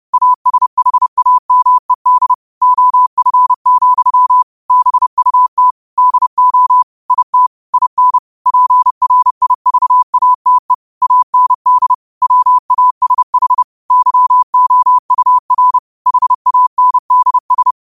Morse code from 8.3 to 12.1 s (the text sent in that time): PRIVATE AND